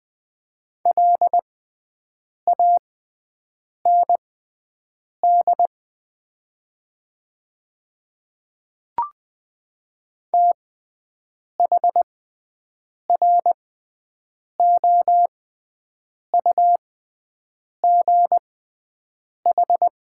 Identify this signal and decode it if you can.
Morse code, heard as a keyed tone, about 20 wpm, spaced as for 7 wpm, 700 Hz. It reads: LAND THROUGH
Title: 6x Character Spacing - Top 200 Words - 20wpm